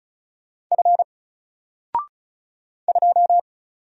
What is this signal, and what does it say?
Morse code, heard as a keyed tone, about 36 wpm, 700 Hz. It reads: F 2
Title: Single Letter-Number 35wpm